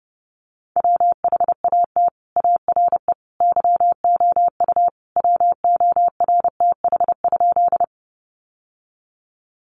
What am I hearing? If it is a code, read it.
Morse code, heard as a keyed tone, about 29 wpm, 700 Hz. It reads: WHAT ARE YOU WORTH?
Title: Sentences from Top 800 Words 30wpm